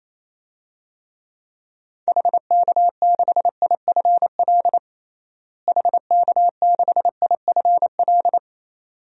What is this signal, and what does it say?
Morse code, heard as a keyed tone, about 28 wpm, 700 Hz. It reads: HK6IFL HK6IFL
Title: DXCC-100 Call Signs - 6 Characters - Repeated 2x - 28wpm